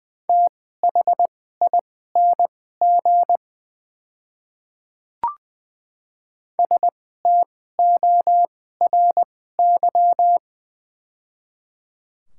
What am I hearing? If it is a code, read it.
Morse code, heard as a keyed tone, about 20 wpm, spaced as for 14 wpm, 700 Hz. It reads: THING STORY
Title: 2x Character Spacing - Top 500 Words - 20wpm